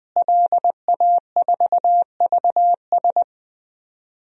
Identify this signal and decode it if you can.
Morse code, heard as a keyed tone, about 20 wpm, 700 Hz. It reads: LA4VS
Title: DXCC-100 Call Signs - 5 Characters - Repeated 2x - 20wpm